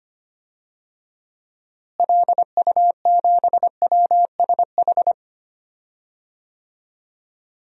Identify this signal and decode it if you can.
Morse code, heard as a keyed tone, about 25 wpm, 700 Hz. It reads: LU7WSH
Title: DXCC-100 Call Signs - 6 Characters - Repeated 3x - 25wpm